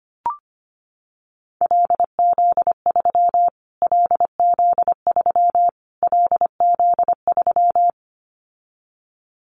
Morse code, heard as a keyed tone, about 25 wpm, 700 Hz. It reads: LZ3 LZ3 LZ3
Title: Callsign Prefixes for Top 25 DXCC - Repeated 3x - 25wpm